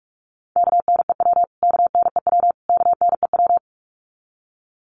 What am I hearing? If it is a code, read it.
Morse code, heard as a keyed tone, about 44 wpm, 700 Hz. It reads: KNEW KNEW KNEW
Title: Top 500 Words Repeated 45wpm